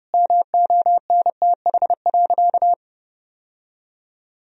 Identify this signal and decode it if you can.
Morse code, heard as a keyed tone, about 32 wpm, 700 Hz. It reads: MONTH.
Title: Sentences from Top 1000 Words - Review Part 3 - 30wpm